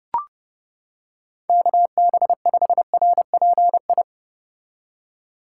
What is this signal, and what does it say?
Morse code, heard as a keyed tone, about 32 wpm, 700 Hz. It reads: KB5RPI